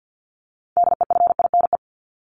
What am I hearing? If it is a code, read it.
Morse code, heard as a keyed tone, about 50 wpm, 700 Hz. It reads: DEFINE
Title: ED Suffix - Top 100 Words with Base Words 50wpm